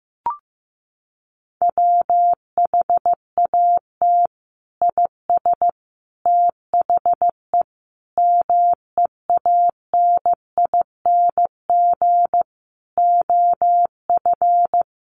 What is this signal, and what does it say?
Morse code, heard as a keyed tone, about 14 wpm, 700 Hz. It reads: WHAT IS THE MEANING OF
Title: Sentences from Top 900 Words 15wpm